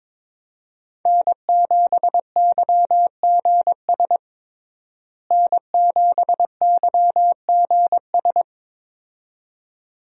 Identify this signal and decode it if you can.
Morse code, heard as a keyed tone, about 22 wpm, 700 Hz. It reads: N7YGS N7YGS
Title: USA Call Signs - 5 Characters - Repeated 2x - 22wpm